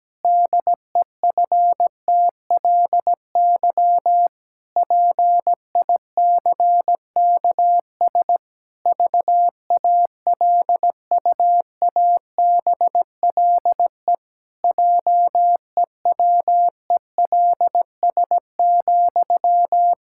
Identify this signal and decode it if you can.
Morse code, heard as a keyed tone, about 17 wpm, 700 Hz. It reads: DEFTLY PICKS VALUABLE JEWELS,
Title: Pangram Challenge v2 - 17wpm